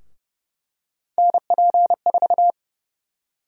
Morse code, heard as a keyed tone, about 28 wpm, 700 Hz. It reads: NP4
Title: USA Call Sign Prefixes - Repeated 2x - 30wpm